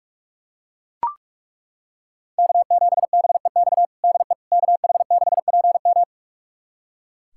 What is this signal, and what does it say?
Morse code, heard as a keyed tone, about 42 wpm, 700 Hz. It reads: K7BEX DE KH6PM
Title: USA Callsign DE Callsign - 5 Characters - 45wpm